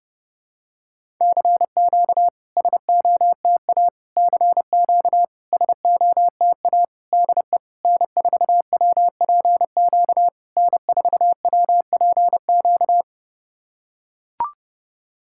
Morse code, heard as a keyed tone, about 32 wpm, 690 Hz. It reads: CQ SOTA CQ SOTA DE N4WPQ N4WPQ
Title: SOTA QSO Practice 30wpm